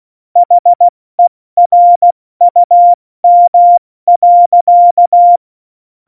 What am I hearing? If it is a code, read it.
Morse code, heard as a keyed tone, about 16 wpm, 700 Hz. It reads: HERUM.
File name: Morsekurs / Witze / Witz 12. Dezember 2017, 70 F80